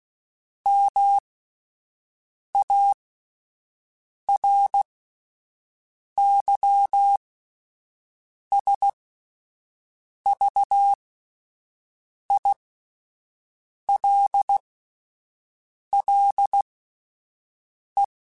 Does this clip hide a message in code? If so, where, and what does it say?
Yes: Morse code from 0.7 to 18.0 s, heard as a keyed tone, about 16 wpm, spaced as for 5 wpm, 800 Hz. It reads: MARYSVILLE